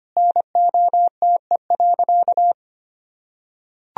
Morse code, heard as a keyed tone, about 25 wpm, 700 Hz. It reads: NOTE.
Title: Sentences from Top 1000 Words - Review Part 2 - 25wpm